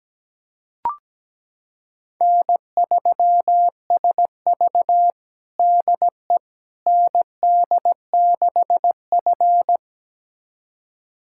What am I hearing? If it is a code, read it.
Morse code, heard as a keyed tone, about 17 wpm, 700 Hz. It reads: N3SV DE ND6F
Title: USA Callsign DE Callsign - 4 Characters - 17wpm